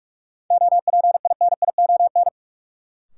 Morse code, heard as a keyed tone, about 46 wpm, 700 Hz. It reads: OPINION
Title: Sentences from Top 1200 Words - New Word Drill - 45wpm